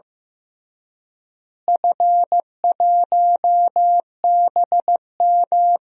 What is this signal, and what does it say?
Morse code, heard as a keyed tone, about 16 wpm, 700 Hz. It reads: F1BM
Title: DXCC-100 Call Signs - 4 Characters - Repeated 3x - 15wpm